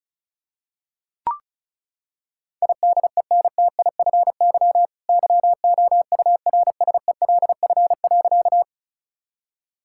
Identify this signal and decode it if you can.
Morse code, heard as a keyed tone, about 36 wpm, 690 Hz. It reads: IDENTIFY YOURSELF.
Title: Sentences from Top 1000 Words - Review Part 6 - 35wpm